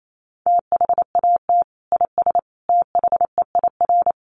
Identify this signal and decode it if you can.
Morse code, heard as a keyed tone, about 28 wpm, 700 Hz. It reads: THAT IS THEIR
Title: Sentences from Top 1400 Words 28wpm